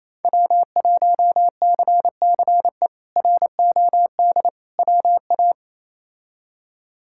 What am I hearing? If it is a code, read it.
Morse code, heard as a keyed tone, about 28 wpm, 700 Hz. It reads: W1CCE ROD WA
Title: K1USN SST QSO Practice - 28wpm